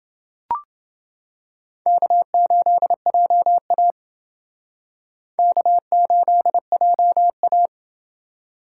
Morse code, heard as a keyed tone, about 28 wpm, 700 Hz. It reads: K8JA K8JA